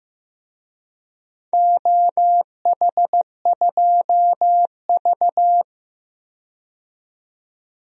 Morse code, heard as a keyed tone, about 15 wpm, 700 Hz. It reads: OH2V
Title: DXCC-100 Call Signs - 4 Characters - 15wpm